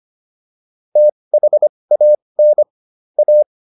Morse code, heard as a keyed tone, about 25 wpm, spaced as for 20 wpm, 600 Hz. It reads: THAN A